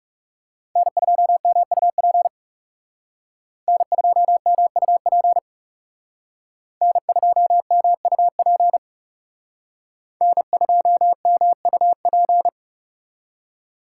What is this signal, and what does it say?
Morse code, heard as a keyed tone, about 38 wpm, 700 Hz. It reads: N2MUP N2MUP N2MUP N2MUP